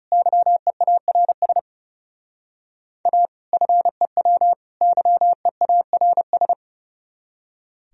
Morse code, heard as a keyed tone, about 40 wpm, 700 Hz. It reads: YEARS A FEW YEARS